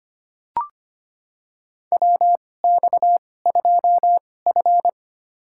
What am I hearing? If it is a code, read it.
Morse code, heard as a keyed tone, about 25 wpm, spaced as for 18 wpm, 700 Hz. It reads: WX2F
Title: USA Call Signs - 4 Characters - 2x Character Spacing - 25wpm